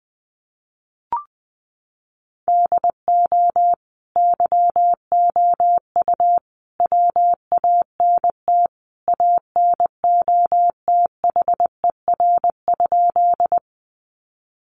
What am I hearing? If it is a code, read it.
Morse code, heard as a keyed tone, about 20 wpm, 700 Hz. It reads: DO YOU WANT ANOTHER?
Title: Sentences from Top 600 Words 20wpm